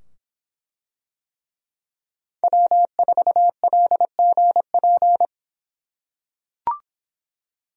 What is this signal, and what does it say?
Morse code, heard as a keyed tone, about 25 wpm, 700 Hz. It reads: W4LGP